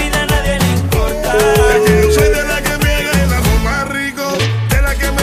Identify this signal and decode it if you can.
Ringback tone (Europe, 425 Hz), with noise.